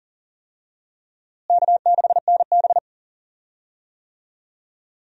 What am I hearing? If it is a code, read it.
Morse code, heard as a keyed tone, about 40 wpm, 700 Hz. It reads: K6NB
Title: USA Call Signs - 4 Characters - Repeated 2x - 40wpm